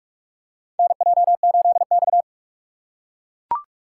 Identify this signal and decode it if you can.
Morse code, heard as a keyed tone, about 44 wpm, 690 Hz. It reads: NJ8X
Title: USA Call Signs - 4x Character Spacing - 45wpm